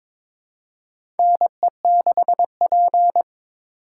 Morse code, heard as a keyed tone, about 22 wpm, 700 Hz. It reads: NE6P
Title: USA Call Signs - 4 Characters - 4x Character Spacing - 22wpm